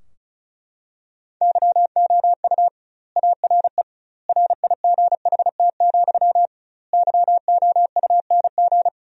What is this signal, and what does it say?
Morse code, heard as a keyed tone, about 36 wpm, 700 Hz. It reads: YOU ARE RIGHT, YOUNG
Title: 2x Word-Spacing - Sentences from Top 400 Words - 35wpm